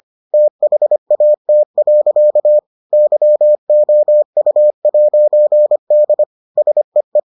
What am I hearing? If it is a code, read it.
Morse code, heard as a keyed tone, about 25 wpm, 600 Hz. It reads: THAT. YOU'D SEE